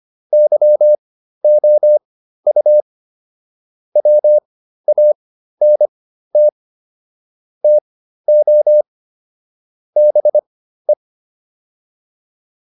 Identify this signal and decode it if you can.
Morse code, heard as a keyed tone, about 25 wpm, spaced as for 13 wpm, 600 Hz. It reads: YOU WANT TO BE
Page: /intermediate-practice-files/